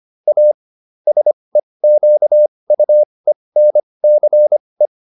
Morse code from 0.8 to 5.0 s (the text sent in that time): SEQUENCE